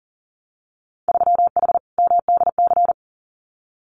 Morse code, heard as a keyed tone, about 38 wpm, 700 Hz. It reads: 3H MDC
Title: Winter Field Day Exchanges Encore 40wpm